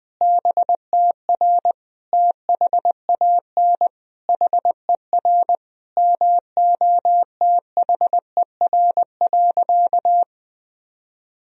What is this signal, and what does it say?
Morse code, heard as a keyed tone, about 20 wpm, 700 Hz. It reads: BTR THAN HER MOTHER.
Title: Sentences from Top 400 Words and Abbreviations 20wpm